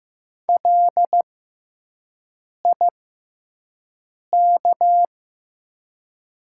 Morse code, heard as a keyed tone, about 15 wpm, spaced as for 5 wpm, 690 Hz. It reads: LIK
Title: 6x Character Spacing - Top 200 Words - 15wpm